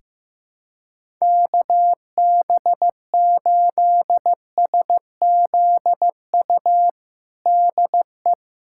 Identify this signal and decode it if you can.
Morse code, heard as a keyed tone, about 15 wpm, 700 Hz. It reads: KB8SZU DE
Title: USA Callsign DE Callsign - 6 Characters - 15wpm